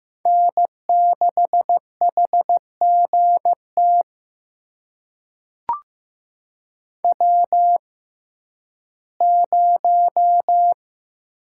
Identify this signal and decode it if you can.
Morse code, heard as a keyed tone, about 15 wpm, spaced as for 11 wpm, 700 Hz. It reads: N6HGT W0
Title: USA Call Signs - 5 Characters - 6x Character Spacing - 15wpm